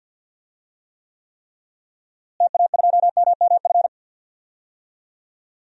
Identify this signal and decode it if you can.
Morse code, heard as a keyed tone, about 48 wpm, 700 Hz. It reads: TA2MMF
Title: DXCC-100 Call Signs - 6 Characters - Repeated 3x - 50wpm